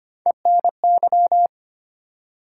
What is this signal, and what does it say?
Morse code, heard as a keyed tone, about 25 wpm, 700 Hz. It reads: ENY
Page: morse-code-ninja-practice-25wpm